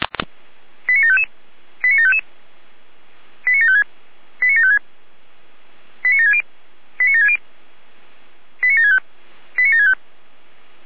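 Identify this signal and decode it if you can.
Five-tone selective call (ZVEI), heard as quick runs of five tones, tones of about 80 ms. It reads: 89755 89755 89756 89756 89760 89760 89765 89765
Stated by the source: ZVEI five-tone call 89725 received on SDR.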